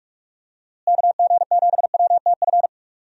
Morse code, heard as a keyed tone, about 44 wpm, 700 Hz. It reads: KG7WTF
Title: USA Call Signs - 6 Characters - Repeated 2x - 45wpm